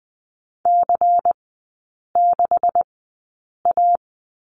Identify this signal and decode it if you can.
Morse code, heard as a keyed tone, about 20 wpm, spaced as for 8 wpm, 690 Hz. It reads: C6A
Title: 133 Lesson 22e - Question Mark - Three Characters 20wpm